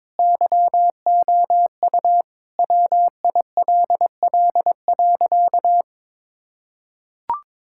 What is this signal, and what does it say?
Morse code, heard as a keyed tone, about 22 wpm, 700 Hz. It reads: YOU WILL.
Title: Sentences from Top 1600 Words 22wpm